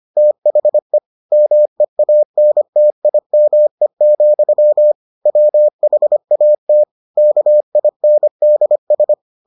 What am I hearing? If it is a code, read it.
Morse code, heard as a keyed tone, about 25 wpm, 600 Hz. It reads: THE MEANTIME, WHAT KINDS